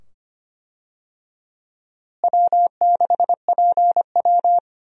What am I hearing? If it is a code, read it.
Morse code, heard as a keyed tone, about 25 wpm, 700 Hz. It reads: W6PW
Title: USA Call Signs - 4 Characters - Repeated 2x - 25wpm